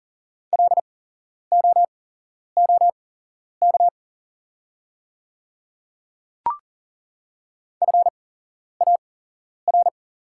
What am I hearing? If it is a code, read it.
Morse code, heard as a keyed tone, about 40 wpm, spaced as for 11 wpm, 690 Hz. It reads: LOOK FAR
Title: 8x Character Spacing - Top 300 Words - 40wpm